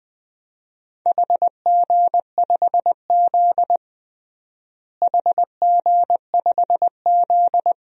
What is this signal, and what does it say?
Morse code, heard as a keyed tone, about 20 wpm, 700 Hz. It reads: HG5Z HG5Z
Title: DXCC-100 Call Signs - 4 Characters - Repeated 2x - 20wpm